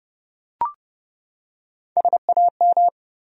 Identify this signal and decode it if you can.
Morse code, heard as a keyed tone, about 31 wpm, 700 Hz. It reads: SAM
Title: USA Common Names - Max 4 Letters - 30wpm